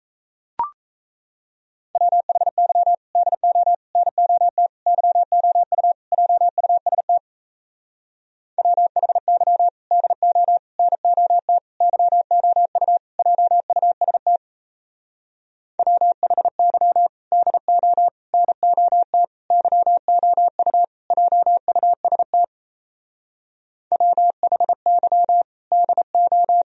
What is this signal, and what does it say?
Morse code, heard as a keyed tone, about 36 wpm, 700 Hz. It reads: WHY DO NOT YOU JUST WHY DO NOT YOU JUST WHY DO NOT YOU JUST WHY DO